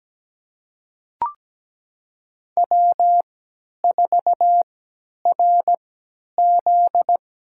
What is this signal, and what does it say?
Morse code, heard as a keyed tone, about 17 wpm, spaced as for 10 wpm, 700 Hz. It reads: W4RZ
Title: USA Call Signs - 3x Character Spacing - 17wpm